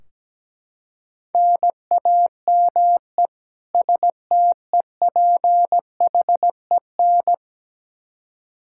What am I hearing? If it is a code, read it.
Morse code, heard as a keyed tone, about 17 wpm, 700 Hz. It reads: NAME STEPHEN
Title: Practice QSO - Exchange 1 - Beginning - 17wpm